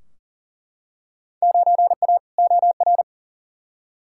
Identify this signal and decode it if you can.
Morse code, heard as a keyed tone, about 40 wpm, 700 Hz. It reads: 9A OR
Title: Field Day Exchanges Encore - 40wpm